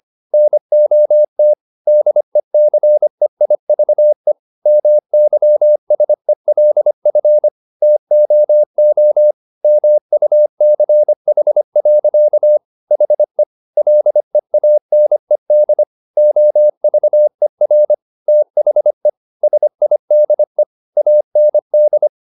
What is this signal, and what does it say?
Morse code, heard as a keyed tone, about 25 wpm, 600 Hz. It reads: NOT DECEIVE MYSELF TOO MUCH. HE LEANED OVER THE SIDE AND